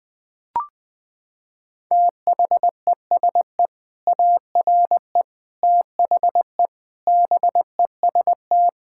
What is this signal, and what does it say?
Morse code, heard as a keyed tone, about 21 wpm, 700 Hz. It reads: THESE ARE THE BEST
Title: Sentences with 2-Word Phrases - 20wpm